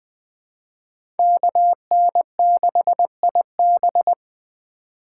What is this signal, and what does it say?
Morse code, heard as a keyed tone, about 20 wpm, 700 Hz. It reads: KN6IB
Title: USA Call Signs - 5 Characters - 4x Character Spacing - 20wpm